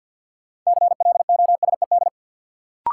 Morse code, heard as a keyed tone, about 51 wpm, 700 Hz. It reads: CLOSED
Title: Sentences from Top 1400 Words - New Word Drill Alt - 50wpm